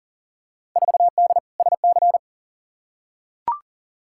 Morse code, heard as a keyed tone, about 40 wpm, 700 Hz. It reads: 4D SC
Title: Field Day Exchanges Encore - 40wpm